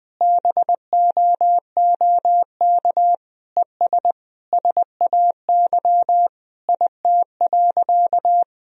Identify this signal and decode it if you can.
Morse code, heard as a keyed tone, about 20 wpm, 700 Hz. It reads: BOOK ES SAY IT.
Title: Sentences from Top 400 Words and Abbreviations 20wpm